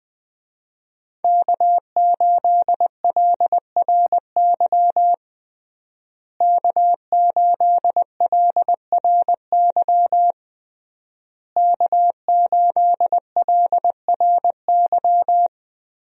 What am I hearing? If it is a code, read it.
Morse code, heard as a keyed tone, about 20 wpm, 700 Hz. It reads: K8LRY K8LRY K8LRY
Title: USA Call Signs - 5 Characters - Repeated 3x - 20wpm